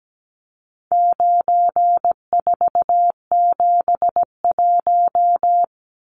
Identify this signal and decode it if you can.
Morse code, heard as a keyed tone, about 17 wpm, 700 Hz. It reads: 9471
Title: Numbers - 4-Digit 17wpm